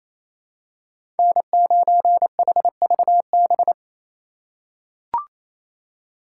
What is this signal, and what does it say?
Morse code, heard as a keyed tone, about 27 wpm, 700 Hz. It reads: N9HVB
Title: USA Call Signs - 3x Character Spacing - 28wpm